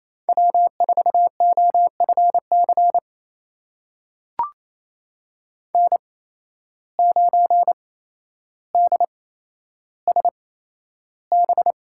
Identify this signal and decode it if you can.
Morse code, heard as a keyed tone, about 28 wpm, spaced as for 12 wpm, 700 Hz. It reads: W4OFC N9DSB
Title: USA Call Signs - 5 Characters - 8x Character Spacing - 28wpm